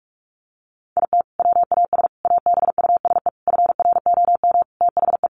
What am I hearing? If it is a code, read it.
Morse code, heard as a keyed tone, about 44 wpm, 700 Hz. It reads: IT WAS ABUSE FROM THE
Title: Sentences from Top 1300 Words 45wpm